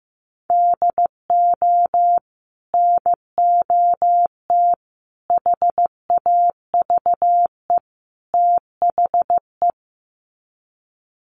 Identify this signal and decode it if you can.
Morse code, heard as a keyed tone, about 15 wpm, 700 Hz. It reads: DO NOT HAVE THE
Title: Sets of 4 Words 15wpm